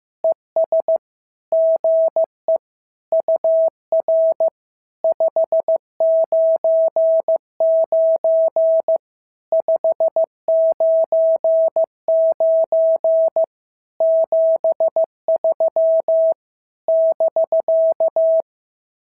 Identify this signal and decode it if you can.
Morse code, heard as a keyed tone, about 15 wpm, 650 Hz. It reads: ES GE UR 599 599 73 <BK>
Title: SOTA QSO Practice 15wpm